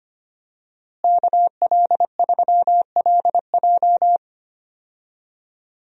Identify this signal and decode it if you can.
Morse code, heard as a keyed tone, about 25 wpm, 700 Hz. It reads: KL3LJ
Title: USA Call Signs - 5 Characters - Repeated 2x - 25wpm